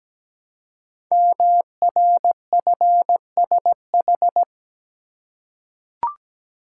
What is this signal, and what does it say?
Morse code, heard as a keyed tone, about 17 wpm, 700 Hz. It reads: MRFSH
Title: Code Groups - 5 Letters - 17wpm